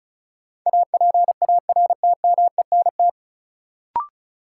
Morse code, heard as a keyed tone, about 34 wpm, 700 Hz. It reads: APARTMENT
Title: Sentences from Top 900 Words - New Word Drill - 35wpm